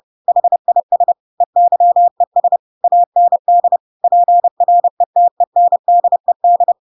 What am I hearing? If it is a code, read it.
Morse code, heard as a keyed tone, about 30 wpm, 700 Hz. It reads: HIS EYES AND PRETENDED
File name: Pinocchio / Pinocchio ch8 30 30 ik1hge